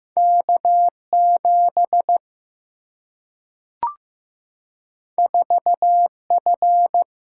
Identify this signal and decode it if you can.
Morse code, heard as a keyed tone, about 15 wpm, 700 Hz. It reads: K 7 4 F
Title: Single-Letter Single-Number 15wpm